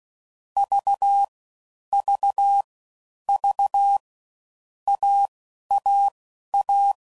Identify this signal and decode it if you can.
Morse code, heard as a keyed tone, about 16 wpm, spaced as for 11 wpm, 800 Hz. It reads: V V V AAA
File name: Combinations / Difficult Combos / AUV Short